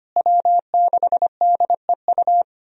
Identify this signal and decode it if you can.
Morse code, heard as a keyed tone, about 25 wpm, 700 Hz. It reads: W6DEU
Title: USA Call Signs - 5 Characters - Repeated 3x - 25wpm